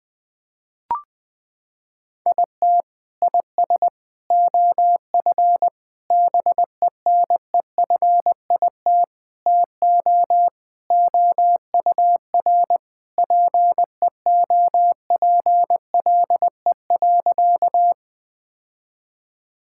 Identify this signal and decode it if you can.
Morse code, heard as a keyed tone, about 20 wpm, 700 Hz. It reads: IT IS OF BENEFIT TO OUR PEOPLE.
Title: Sentences from Top 1000 Words - Review Part 6 - 20wpm